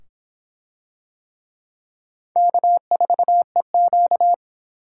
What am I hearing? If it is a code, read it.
Morse code, heard as a keyed tone, about 22 wpm, 700 Hz. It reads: K4EQ